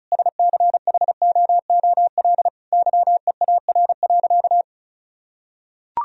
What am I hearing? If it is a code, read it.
Morse code, heard as a keyed tone, about 36 wpm, 700 Hz. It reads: SCHOOL YEAR.
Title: Sentences from Top 1000 Words - Review Part 1 - 35wpm